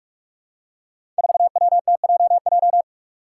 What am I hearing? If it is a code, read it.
Morse code, heard as a keyed tone, about 44 wpm, 700 Hz. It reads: 4WTJJ
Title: Five Challenging Letter-Number Combinations 45wpm